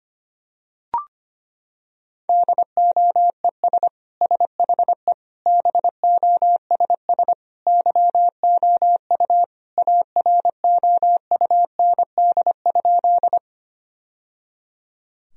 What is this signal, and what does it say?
Morse code, heard as a keyed tone, about 25 wpm, 700 Hz. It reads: DOES SHE BOSS YOU AROUND?
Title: Sentences from Top 1000 Words - Review Part 4 - 25wpm